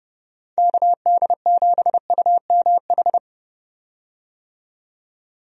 Morse code, heard as a keyed tone, about 30 wpm, 700 Hz. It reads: KD7UMH
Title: USA Call Signs - 6 Characters - 30wpm